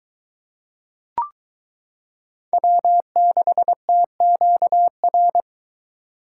Morse code, heard as a keyed tone, about 22 wpm, 700 Hz. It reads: W6TQR